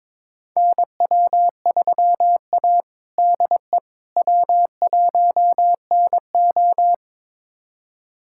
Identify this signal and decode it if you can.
Morse code, heard as a keyed tone, about 21 wpm, 700 Hz. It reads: NW3A DE W1NO
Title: USA Callsign DE Callsign - 4 Characters - 22wpm